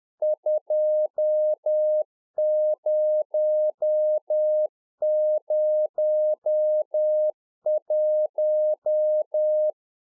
Morse code, heard as a keyed tone, about 10 wpm, 610 Hz. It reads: 2001
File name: 2011-04-mp3 / UNFOGETTABLE QSO 10WPM